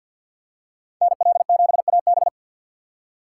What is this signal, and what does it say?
Morse code, heard as a keyed tone, about 51 wpm, 700 Hz. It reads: NL7AB
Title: USA Call Signs - 4x Character Spacing - 50wpm